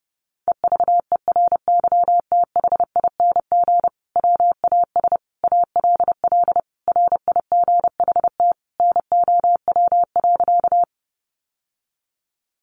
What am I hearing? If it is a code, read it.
Morse code, heard as a keyed tone, about 30 wpm, 700 Hz. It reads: EVERYTHING WAS ALL RIGHT NOW.
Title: Sentences from Top 700 Words 30wpm